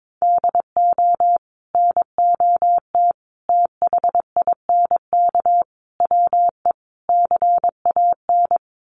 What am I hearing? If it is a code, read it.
Morse code, heard as a keyed tone, about 22 wpm, 700 Hz. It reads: DO NOT THINK WE CAN
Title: Sets of 5 Words Encore2 22wpm